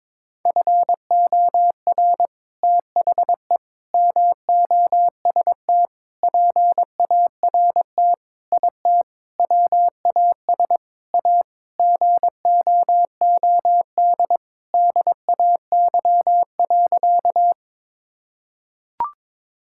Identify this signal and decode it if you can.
Morse code, heard as a keyed tone, about 22 wpm, 700 Hz. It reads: FOR THE MOST PART IT WAS A GOOD DAY.
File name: practice / 22wpm / Sentences with 4-Word Phrases - 22wpm